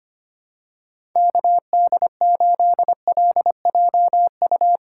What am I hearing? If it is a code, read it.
Morse code, heard as a keyed tone, about 25 wpm, 700 Hz. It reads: KD8LJU
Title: USA Call Signs - 6 Characters - 8x Character Spacing - 25wpm